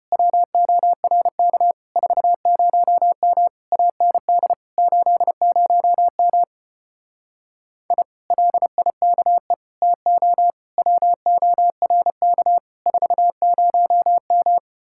Morse code, heard as a keyed tone, about 33 wpm, 700 Hz. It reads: WORK 40M AND 80M I LIKE TO WORK 40M